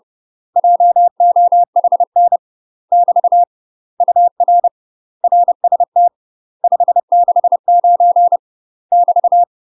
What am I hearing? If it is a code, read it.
Morse code, heard as a keyed tone, about 30 wpm, 700 Hz. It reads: JOHN <BT> UR RST 569 <BT>
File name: UK MASTER QSOs TRAINING / 30WPM / UK MASTER QSOs 0039